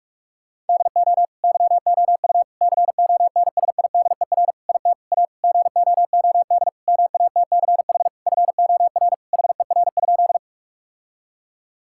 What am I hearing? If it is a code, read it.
Morse code, heard as a keyed tone, about 46 wpm, 700 Hz. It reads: DO YOU CONSIDER IT A GOOD MATCH FOR HER?
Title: Sentences from Top 1600 Words 45wpm